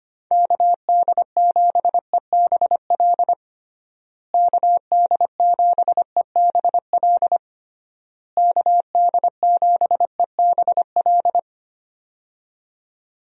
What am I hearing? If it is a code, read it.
Morse code, heard as a keyed tone, about 25 wpm, 700 Hz. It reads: KD7EBL KD7EBL KD7EBL
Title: USA Call Signs - 6 Characters - Repeated 3x - 25wpm